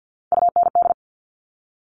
Morse code, heard as a keyed tone, about 50 wpm, 690 Hz. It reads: UND
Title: Top Three Letter Combinations in Words 50wpm